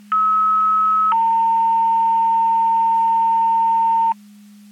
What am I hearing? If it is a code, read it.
Two-tone page, heard as a one-second tone, then a three-second tone, about 1290 then 910 Hz, with noise. It is heard from 0.1 to 4.1 s.
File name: Portage / 11-00-9651 Portage 154.385 Paging